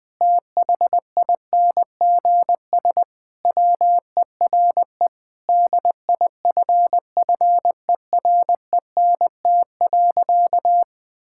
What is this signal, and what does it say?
Morse code, heard as a keyed tone, about 21 wpm, 700 Hz. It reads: THINGS WERE DIFFERENT.
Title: Sentences with 6-Word Phrases - 20wpm